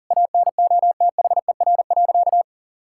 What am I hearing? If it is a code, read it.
Morse code, heard as a keyed tone, about 40 wpm, 700 Hz. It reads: ANOTHER.
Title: Sentences from Top 1000 Words - Review Part 6 - 40wpm